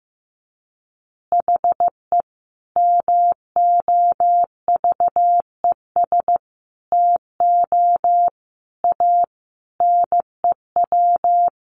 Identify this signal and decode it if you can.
Morse code, heard as a keyed tone, about 15 wpm, 700 Hz. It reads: HE MOVES TO A NEW
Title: Easy Phrases 15wpm